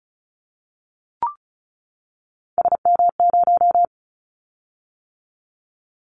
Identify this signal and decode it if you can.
Morse code, heard as a keyed tone, about 34 wpm, 700 Hz. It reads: SM0
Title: Callsign Prefixes for Top 25 DXCC - 35wpm